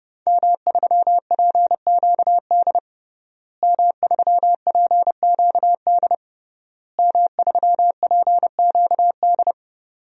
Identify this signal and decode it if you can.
Morse code, heard as a keyed tone, about 30 wpm, 700 Hz. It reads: M3PQD M3PQD M3PQD
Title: DXCC-100 Call Signs - 5 Characters - Repeated 3x - 30wpm